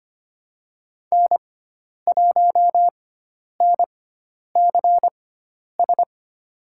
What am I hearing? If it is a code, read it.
Morse code, heard as a keyed tone, about 25 wpm, spaced as for 10 wpm, 700 Hz. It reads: N1NCS
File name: Lesson 16g - Letter c - Call Signs 25wpm